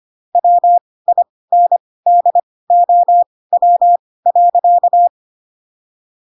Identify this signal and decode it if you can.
Morse code, heard as a keyed tone, about 25 wpm, spaced as for 18 wpm, 700 Hz. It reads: WINDOW.
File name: Pinocchio / Pinocchio ch7 12 25 ik1hge